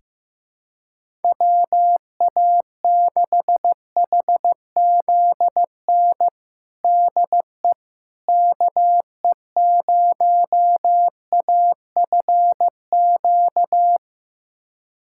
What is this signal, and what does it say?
Morse code, heard as a keyed tone, about 16 wpm, 700 Hz. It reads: WA6HZN DE KE0AFQ
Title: USA Callsign DE Callsign - 6 Characters - 15wpm